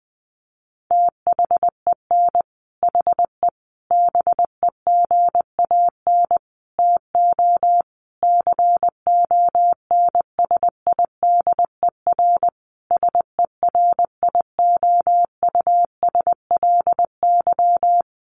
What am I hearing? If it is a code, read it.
Morse code, heard as a keyed tone, about 20 wpm, 700 Hz. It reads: THEN HE BEGAN TO CONSIDER SERIOUSLY
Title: Sentences from Top 1200 Words 20wpm